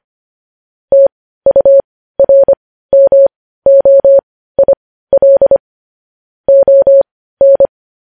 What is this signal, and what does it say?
Morse code, heard as a keyed tone, about 25 wpm, spaced as for 15 wpm, 550 Hz. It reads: TURMOIL ON
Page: News Headlines in Morse Code at 15 WPM